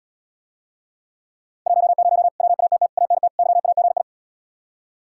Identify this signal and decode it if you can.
Morse code, heard as a keyed tone, about 79 wpm, 700 Hz. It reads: 10 WATT ANT YAGI